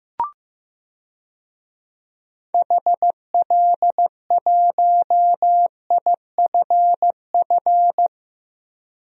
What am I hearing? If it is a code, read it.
Morse code, heard as a keyed tone, about 15 wpm, 700 Hz. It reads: HL1IFF
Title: DXCC-100 Call Signs - 6 Characters - 15wpm